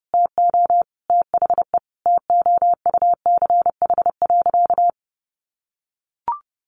Morse code, heard as a keyed tone, about 30 wpm, 700 Hz. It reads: TO THE TOUCH.
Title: Sentences from Top 1500 Words 30wpm